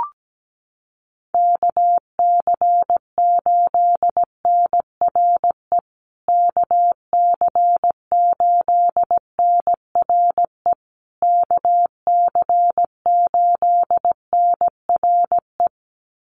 Morse code, heard as a keyed tone, about 17 wpm, 700 Hz. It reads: KC8NRE KC8NRE KC8NRE
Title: USA Call Signs Repeated 3x 17wpm